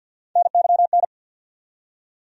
Morse code, heard as a keyed tone, about 47 wpm, 700 Hz. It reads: NYN
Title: USA Call Sign Postfixes - 3 Letters 50wpm